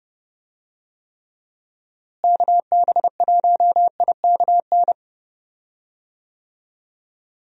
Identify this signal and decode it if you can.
Morse code, heard as a keyed tone, about 29 wpm, 700 Hz. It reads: KB1IKN
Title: USA Call Signs - 6 Characters - Repeated 2x - 30wpm